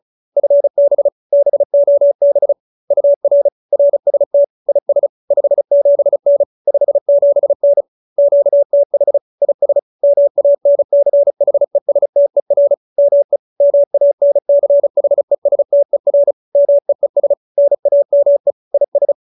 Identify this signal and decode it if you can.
Morse code, heard as a keyed tone, about 34 wpm, 570 Hz. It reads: FB BOB UR RST IS 57N 57N QTH IS MANCHESTER ME MANCHESTER MEES NAME IS
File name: QSO313 35